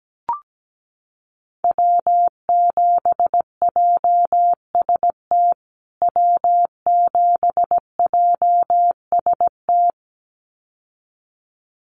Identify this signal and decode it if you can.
Morse code, heard as a keyed tone, about 17 wpm, 700 Hz. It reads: W7JST W7JST
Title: USA Call Signs Repeated 2x 17wpm